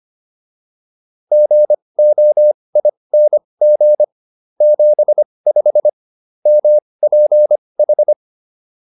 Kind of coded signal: Morse code

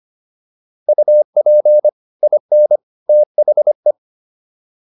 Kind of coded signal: Morse code

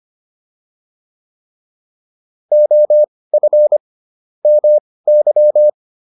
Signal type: Morse code

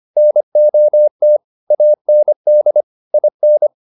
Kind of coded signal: Morse code

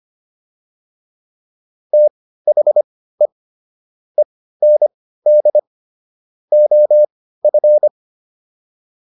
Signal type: Morse code

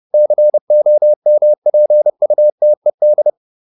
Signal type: Morse code

